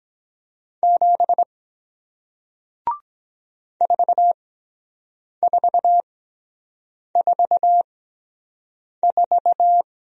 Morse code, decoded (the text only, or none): 7 4444